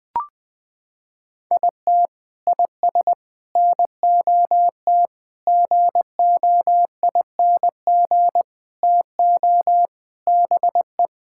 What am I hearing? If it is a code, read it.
Morse code, heard as a keyed tone, about 20 wpm, 700 Hz. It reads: IT IS NOT GOING TO BE
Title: Sentences with 4-Word Phrases - 20wpm